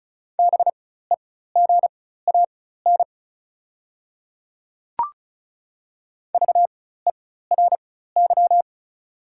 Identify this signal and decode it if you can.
Morse code, heard as a keyed tone, about 36 wpm, spaced as for 16 wpm, 700 Hz. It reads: BEGAN VERY
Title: 4x Character Spacing - Top 500 Words - 35wpm